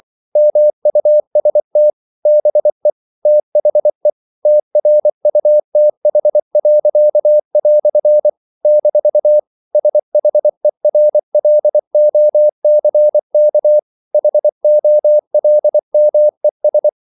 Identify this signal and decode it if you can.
Morse code, heard as a keyed tone, about 24 wpm, 600 Hz. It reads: MUST BE THE TRUTH." - SHERLOCK HOLMES
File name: CW / Practice / EnglishQuotations0224